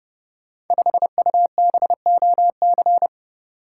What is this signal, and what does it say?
Morse code, heard as a keyed tone, about 29 wpm, 700 Hz. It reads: 5UBOC
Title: Code Groups - 5 Letter:Numbers - 30wpm